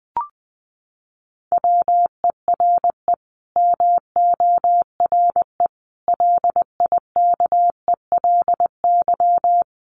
Morse code, heard as a keyed tone, about 20 wpm, 700 Hz. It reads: WERE MORE LIKELY
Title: Sets of 5 Words Encore2 20wpm